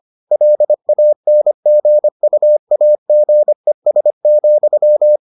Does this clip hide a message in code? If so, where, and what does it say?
Yes: Morse code from 0.3 to 5.2 s, heard as a keyed tone, about 25 wpm, 600 Hz. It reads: LANGUAGES,